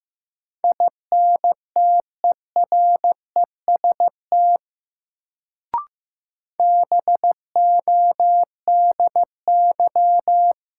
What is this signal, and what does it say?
Morse code, heard as a keyed tone, about 15 wpm, 700 Hz. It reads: INTEREST BODY
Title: Sentences from Top 400 Words - New Word Drill - 15wpm